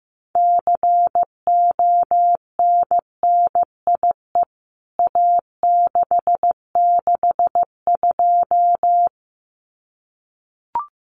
Morse code, heard as a keyed tone, about 15 wpm, 700 Hz. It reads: CONNIE A662
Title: CWT Exchange 15wpm